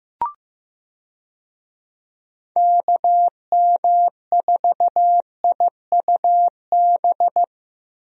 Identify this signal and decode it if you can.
Morse code, heard as a keyed tone, about 15 wpm, 700 Hz. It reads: KM4IUB
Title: USA Call Signs - 6 Characters - 15wpm